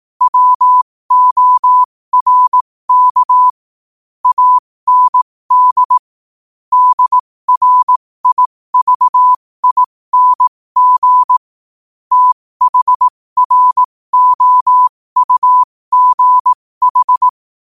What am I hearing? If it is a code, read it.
Morse code, heard as a keyed tone, about 18 wpm, spaced as for 15 wpm, 1000 Hz. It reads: WORK AND DRIVING THROUGH
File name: qotdfiles / 15wpm